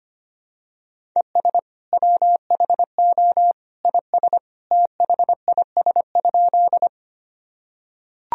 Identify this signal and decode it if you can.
Morse code, heard as a keyed tone, about 25 wpm, 700 Hz. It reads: ES WHO IS THIS?
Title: Sentences from Top 100 Words and Abbreviations 25wpm